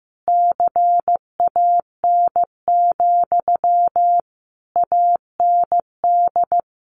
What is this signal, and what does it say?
Morse code, heard as a keyed tone, about 15 wpm, 700 Hz. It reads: CAN, AND
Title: Sentences from Top 1300 Words 15wpm